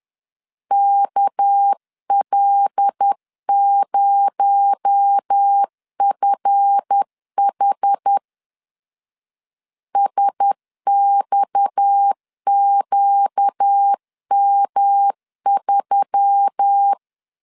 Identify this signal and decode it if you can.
Morse code, heard as a keyed tone, about 10 wpm, 800 Hz. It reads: KL0FH SXQM3
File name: LEZIONI / 7 GF9Z / 7 GF 10wpm 2514